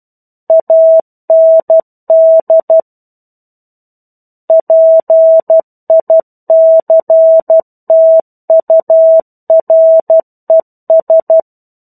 Morse code, heard as a keyed tone, about 12 wpm, 650 Hz. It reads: AND PICTURES
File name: 01-Chapter-01 12 WPM